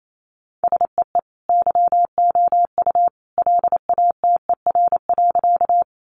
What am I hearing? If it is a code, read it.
Morse code, heard as a keyed tone, about 28 wpm, 700 Hz. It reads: SEE YOU LATER.
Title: Sentences from Top 700 Words 28wpm